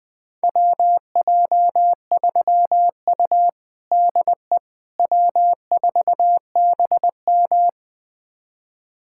Morse code, heard as a keyed tone, about 20 wpm, 700 Hz. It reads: WJ3U DE W4BM
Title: USA Callsign DE Callsign - 4 Characters - 20wpm